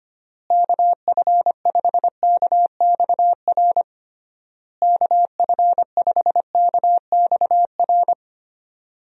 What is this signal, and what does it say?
Morse code, heard as a keyed tone, about 25 wpm, 700 Hz. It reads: KF5KXR KF5KXR
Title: USA Call Signs - 6 Characters - Repeated 3x - 25wpm